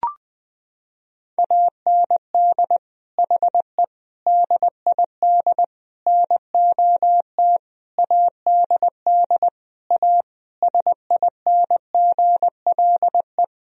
Morse code, heard as a keyed tone, about 20 wpm, 700 Hz. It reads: AND HE DID NOT ADD A SINGLE
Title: Sentences from Top 1000 Words - Review Part 1 - 20wpm